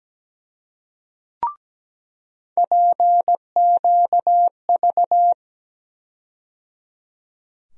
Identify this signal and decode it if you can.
Morse code, heard as a keyed tone, about 17 wpm, 700 Hz. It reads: PQV
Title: USA Call Sign Postfixes - 3 Letters 17wpm